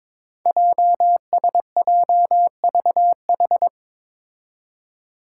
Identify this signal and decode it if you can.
Morse code, heard as a keyed tone, about 22 wpm, 700 Hz. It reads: JSJVH
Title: Five Challenging Letter-Number Combinations 22wpm